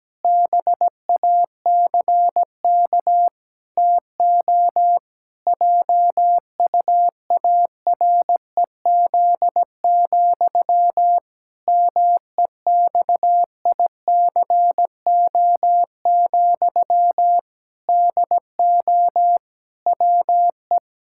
Morse code, heard as a keyed tone, about 17 wpm, 700 Hz. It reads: BACK TO JUAREZ, MEXICO, DO WE
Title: Pangram Challenge v2 - 17wpm